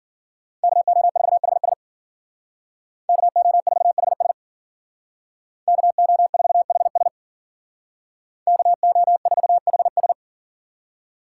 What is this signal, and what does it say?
Morse code, heard as a keyed tone, about 52 wpm, 700 Hz. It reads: KO4HS KO4HS KO4HS KO4HS